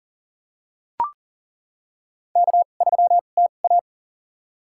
Morse code, heard as a keyed tone, about 40 wpm, spaced as for 29 wpm, 700 Hz. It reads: K3TA
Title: USA Call Signs - 2x Character Spacing - 40wpm